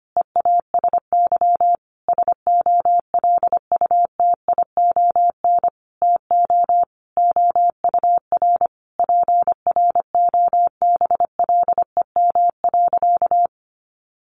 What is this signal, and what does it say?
Morse code, heard as a keyed tone, about 25 wpm, 700 Hz. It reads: EASY SOLUTION TO OUR PROBLEM.
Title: Sentences from Top 1300 Words 25wpm